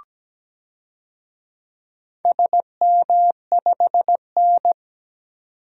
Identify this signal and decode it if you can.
Morse code, heard as a keyed tone, about 17 wpm, 700 Hz. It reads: SM5N
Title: DXCC-100 Call Signs - 4 Characters - 17wpm